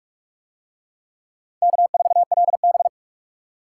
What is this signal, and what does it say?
Morse code, heard as a keyed tone, about 41 wpm, 700 Hz. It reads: K4LB